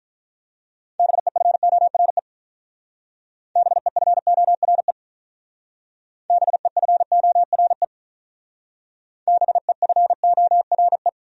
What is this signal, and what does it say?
Morse code, heard as a keyed tone, about 47 wpm, 700 Hz. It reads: BEFORE BEFORE BEFORE BEFORE